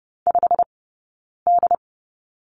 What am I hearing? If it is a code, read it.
Morse code, heard as a keyed tone, about 30 wpm, spaced as for 9 wpm, 690 Hz. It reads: 5D
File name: course / 30:10wpm / 086 Lesson 15d - Number 5 - Two Characters 30wpm